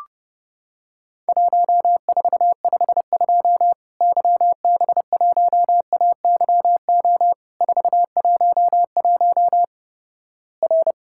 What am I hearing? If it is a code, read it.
Morse code, heard as a keyed tone, about 32 wpm, 700 Hz. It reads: 1452 YB1AYO 411 R
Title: Worked All Europe DX Contest - QTC Exchange - 30wpm